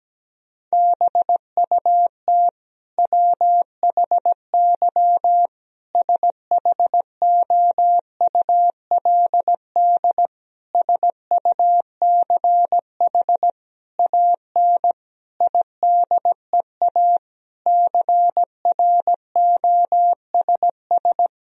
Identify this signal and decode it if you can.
Morse code, heard as a keyed tone, about 17 wpm, 700 Hz. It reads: BUT WHY SHOULD SUCH AN IDEA CROSS
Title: Sentences from Top 1000 Words - Review Part 1 - 17wpm